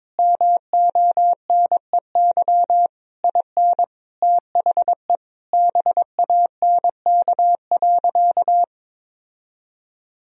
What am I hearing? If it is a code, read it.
Morse code, heard as a keyed tone, about 22 wpm, 700 Hz. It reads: MONEY IN THE BANK.
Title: Sentences from Top 1000 Words - Review Part 4 - 22wpm